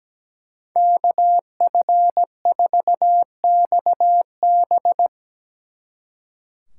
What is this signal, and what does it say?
Morse code, heard as a keyed tone, about 19 wpm, 700 Hz. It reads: KF4XB
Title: Speed Racing - USA 2x2 Call Signs 17wpm